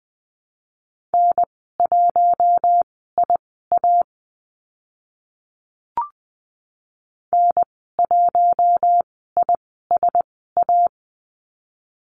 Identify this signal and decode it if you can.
Morse code, heard as a keyed tone, about 20 wpm, spaced as for 14 wpm, 700 Hz. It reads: N1IA N1ISA
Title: 042 Lesson 8g - Number 1 - Call Signs 20wpm